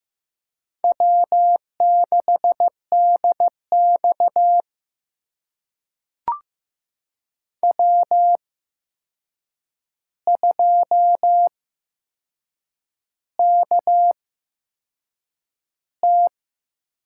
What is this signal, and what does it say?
Morse code, heard as a keyed tone, about 15 wpm, spaced as for 6 wpm, 700 Hz. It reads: W6DX W2KT
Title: USA Call Signs - 4 Characters - 8x Character Spacing - 15wpm